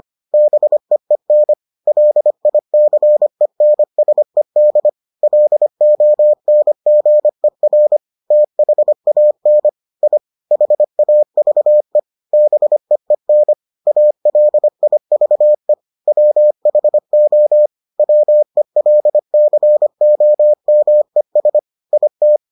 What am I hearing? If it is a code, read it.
Morse code, heard as a keyed tone, about 25 wpm, 600 Hz. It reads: BEEN LICENSED LONGER THAN I HAVE BEEN ALIVE WHO WELCOMES IT